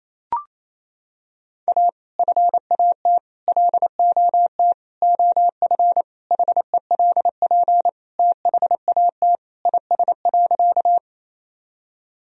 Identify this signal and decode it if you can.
Morse code, heard as a keyed tone, about 28 wpm, 700 Hz. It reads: A FAT LOT OF HELP THAT IS.
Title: Sentences from Top 1000 Words - Review Part 4 - 28wpm